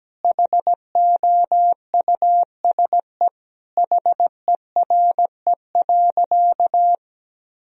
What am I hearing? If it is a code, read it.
Morse code, heard as a keyed tone, about 17 wpm, 700 Hz. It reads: HOUSE HERE.
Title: Sentences from Top 1000 Words - Review Part 1 - 17wpm